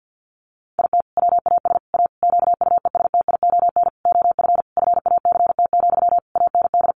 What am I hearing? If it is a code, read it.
Morse code, heard as a keyed tone, about 50 wpm, 700 Hz. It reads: IT WAS A QUESTION OF FACT, AND
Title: Sentences from Top 1400 Words 50wpm